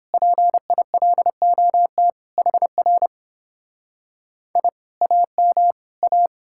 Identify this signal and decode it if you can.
Morse code, heard as a keyed tone, about 29 wpm, 700 Hz. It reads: PILOT HR I AM A